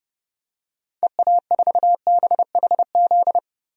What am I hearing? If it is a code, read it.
Morse code, heard as a keyed tone, about 30 wpm, 700 Hz. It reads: EA4BHZ
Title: DXCC-100 Call Signs - 6 Characters - 30wpm